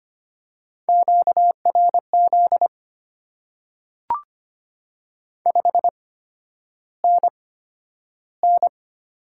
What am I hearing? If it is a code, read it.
Morse code, heard as a keyed tone, about 25 wpm, spaced as for 11 wpm, 690 Hz. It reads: QRZ 5NN
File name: Top 90 QSO Elements - 8x Character Spacing - 25wpm